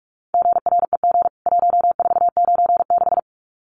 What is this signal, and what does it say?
Morse code, heard as a keyed tone, about 44 wpm, 700 Hz. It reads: GREG 1496
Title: CWT Exchange 45wpm